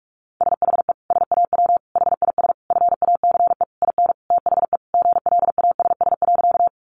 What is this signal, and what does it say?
Morse code, heard as a keyed tone, about 44 wpm, 700 Hz. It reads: SHE SAW HIS FACE IN THE GLASS.
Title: Sentences from Top 1300 Words 45wpm